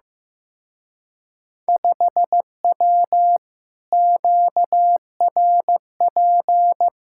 Morse code, heard as a keyed tone, about 16 wpm, 700 Hz. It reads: 5W QRP
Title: Speed-Racing Common QSO Phrases Encore - 15wpm